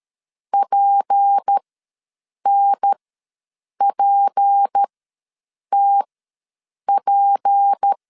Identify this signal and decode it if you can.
Morse code, heard as a keyed tone, about 13 wpm, spaced as for 7 wpm, 800 Hz. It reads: PNPTP
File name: LEZIONI / 6 PKX0 / 6 P ws 12wpm 3117